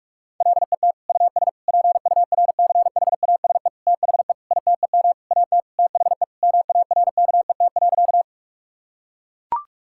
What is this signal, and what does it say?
Morse code, heard as a keyed tone, about 42 wpm, 700 Hz. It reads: LET US PURCHASE THE ITEM AT THE MARKET.
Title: Sentences from Top 1000 Words - Review Part 6 - 45wpm